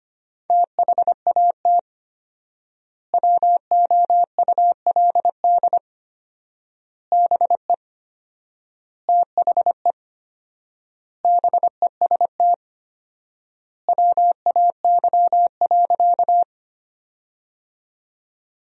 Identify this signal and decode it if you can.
Morse code, heard as a keyed tone, about 25 wpm, 700 Hz. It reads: THAT WOULD BE THE BEST WAY.
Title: 4x Word-Spacing - Sentences from Top 400 Words - 25wpm